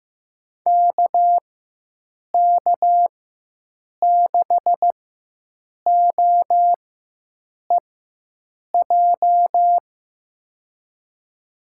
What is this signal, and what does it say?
Morse code, heard as a keyed tone, about 15 wpm, spaced as for 7 wpm, 700 Hz. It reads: KK6OEJ
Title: USA Call Signs - 6 Characters - 4x Character Spacing - 15wpm